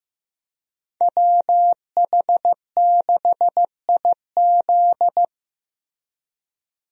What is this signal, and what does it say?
Morse code, heard as a keyed tone, about 15 wpm, 700 Hz. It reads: WH6IZ
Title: USA Call Signs - 5 Characters - Repeated 2x - 15wpm